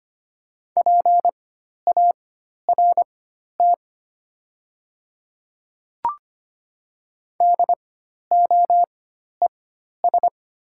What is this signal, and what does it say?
Morse code, heard as a keyed tone, about 25 wpm, spaced as for 12 wpm, 700 Hz. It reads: PART DOES
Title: 4x Character Spacing - Top 200 Words - 25wpm